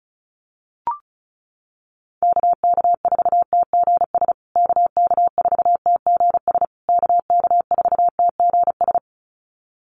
Morse code, heard as a keyed tone, about 34 wpm, 700 Hz. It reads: KK4TGS KK4TGS KK4TGS
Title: USA Call Signs Repeated 3x 35wpm